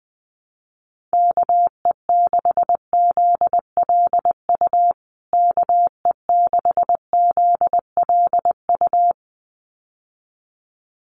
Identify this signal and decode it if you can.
Morse code, heard as a keyed tone, about 20 wpm, 700 Hz. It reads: KE6ZLU KE6ZLU
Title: USA Call Signs Repeated 2x 20wpm